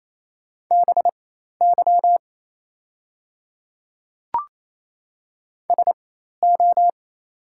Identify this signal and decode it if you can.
Morse code, heard as a keyed tone, about 27 wpm, spaced as for 13 wpm, 690 Hz. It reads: BY SO
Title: 4x Character Spacing - Top 100 Words - 28wpm